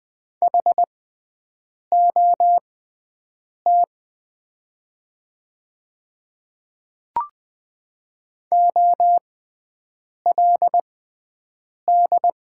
Morse code, heard as a keyed tone, about 20 wpm, spaced as for 7 wpm, 700 Hz. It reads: HOT OLD
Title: 6x Character Spacing - Top 300 Words - 20wpm